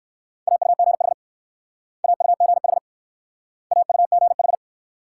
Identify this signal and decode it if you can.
Morse code, heard as a keyed tone, about 56 wpm, 700 Hz. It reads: AUGH AUGH AUGH
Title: Speed-Racing Multi-letter Phonemes - 45wpm